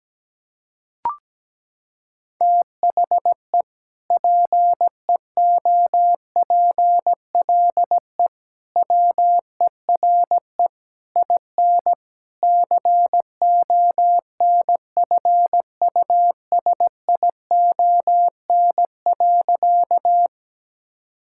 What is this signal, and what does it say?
Morse code, heard as a keyed tone, about 17 wpm, 700 Hz. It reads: THE PEOPLE WERE IN CONFUSION.
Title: Sentences from Top 1600 Words 17wpm